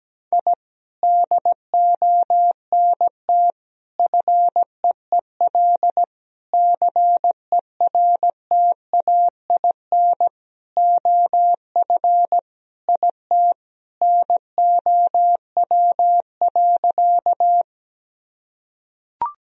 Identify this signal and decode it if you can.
Morse code, heard as a keyed tone, about 17 wpm, 700 Hz. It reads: I DONT FEEL CERTAIN OF IT NOW.
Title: Sentences from Top 1000 Words - Review Part 2 - 17wpm